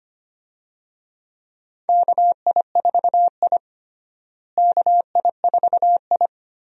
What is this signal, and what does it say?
Morse code, heard as a keyed tone, about 25 wpm, 700 Hz. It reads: KI4I KI4I
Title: USA Call Signs - 4 Characters - Repeated 2x - 25wpm